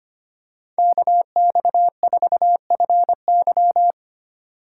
Morse code, heard as a keyed tone, about 25 wpm, 700 Hz. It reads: KX4FY
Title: USA Call Signs - 5 Characters - 4x Character Spacing - 25wpm